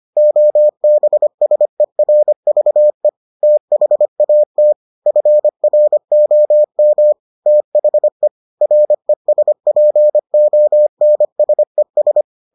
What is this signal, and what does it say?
Morse code, heard as a keyed tone, about 25 wpm, 600 Hz. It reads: OBSERVE THAT FROM THE RESPONSES